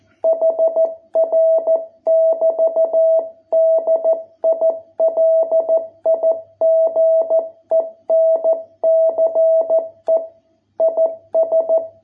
Morse code, heard as a keyed tone, about 14 wpm, 660 Hz. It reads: HR <BT> DILIGENCE IS